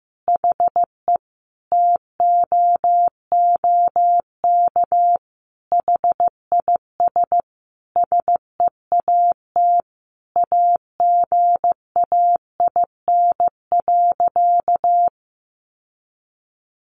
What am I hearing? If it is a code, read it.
Morse code, heard as a keyed tone, about 14 wpm, 700 Hz. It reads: HE TOOK HIS SEAT AGAIN.
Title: Sentences from Top 1500 Words 15wpm